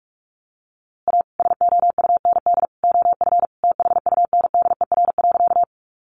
Morse code, heard as a keyed tone, about 44 wpm, 700 Hz. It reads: A SOUND OF THUNDER.
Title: A Sound Of Thunder 45wpm